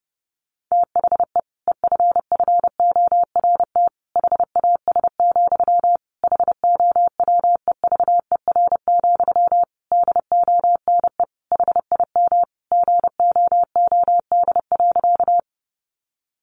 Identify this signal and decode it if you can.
Morse code, heard as a keyed tone, about 30 wpm, 700 Hz. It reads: THE EFFORT HAS, HOWEVER, DONE HIM GOOD.
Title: Sentences from Top 1200 Words 30wpm